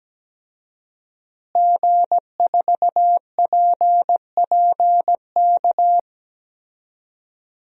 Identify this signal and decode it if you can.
Morse code, heard as a keyed tone, about 17 wpm, 700 Hz. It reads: G4PPK
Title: DXCC-100 Call Signs - 5 Characters - Repeated 2x - 17wpm